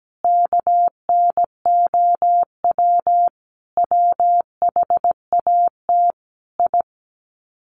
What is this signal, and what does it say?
Morse code, heard as a keyed tone, about 17 wpm, 700 Hz. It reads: KNOW WHAT I
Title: Sets of 5 Words 17wpm